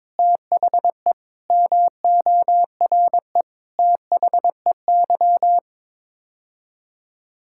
Morse code, heard as a keyed tone, about 23 wpm, 700 Hz. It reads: THE MORE THEY
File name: practice / 22wpm / Sets of 7 Words - 22wpm